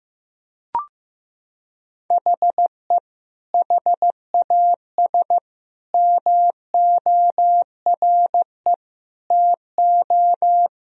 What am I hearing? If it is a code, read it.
Morse code, heard as a keyed tone, about 15 wpm, 700 Hz. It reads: HE HAS MORE TO
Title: Sentences from Top 200 Words and Abbreviations 15wpm